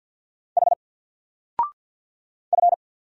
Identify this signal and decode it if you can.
Morse code, heard as a keyed tone, about 50 wpm, 700 Hz. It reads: H F
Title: Single Letter-Number Alt 50wpm